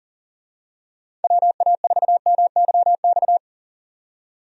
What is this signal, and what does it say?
Morse code, heard as a keyed tone, about 37 wpm, 700 Hz. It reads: WA4MYX
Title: USA Call Signs - 6 Characters - 40wpm